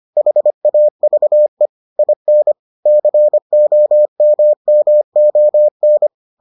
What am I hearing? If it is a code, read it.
Morse code, heard as a keyed tone, about 25 wpm, 600 Hz. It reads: HAVE IN COMMON